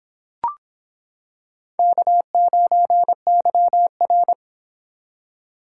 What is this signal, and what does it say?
Morse code, heard as a keyed tone, about 26 wpm, 700 Hz. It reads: K9YR